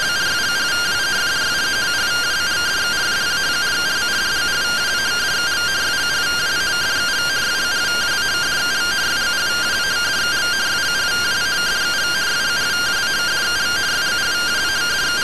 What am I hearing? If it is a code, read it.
Two-tone teleprinter signal, about 170 Hz shift.